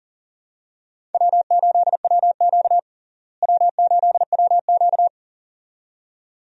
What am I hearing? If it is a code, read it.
Morse code, heard as a keyed tone, about 40 wpm, 700 Hz. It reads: W8WQ W8WQ
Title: USA Call Signs - 4 Characters - Repeated 2x - 40wpm